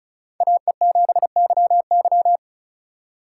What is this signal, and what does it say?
Morse code, heard as a keyed tone, about 34 wpm, 700 Hz. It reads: AE7YY
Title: USA Call Signs - 5 Characters - 2x Character Spacing - 35wpm